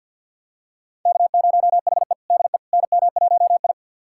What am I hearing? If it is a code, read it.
Morse code, heard as a keyed tone, about 46 wpm, 700 Hz. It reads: K0HE DE NM1I
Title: USA Callsign DE Callsign - 4 Characters - 50wpm